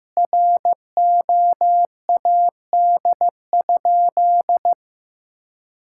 Morse code, heard as a keyed tone, about 15 wpm, 700 Hz. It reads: ROAD?
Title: Sentences from Top 1000 Words - Review Part 5 - 15wpm